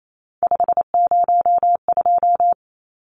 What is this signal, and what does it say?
Morse code, heard as a keyed tone, about 28 wpm, 700 Hz. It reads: 502
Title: Numbers - 3-Digit 28wpm